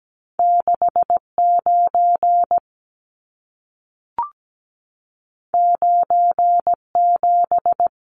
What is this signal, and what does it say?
Morse code, heard as a keyed tone, about 17 wpm, 700 Hz. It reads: 69 97
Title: Numbers - 2-Digit 17wpm